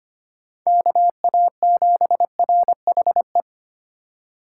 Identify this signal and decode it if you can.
Morse code, heard as a keyed tone, about 30 wpm, 700 Hz. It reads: KA7RHE
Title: Speed Racing - USA 2x3 Call Signs 25wpm